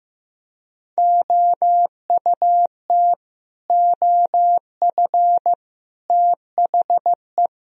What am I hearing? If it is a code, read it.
Morse code, heard as a keyed tone, about 15 wpm, 700 Hz. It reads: OUT OF THE
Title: Speed-Racing - Sets of 5 Words 15wpm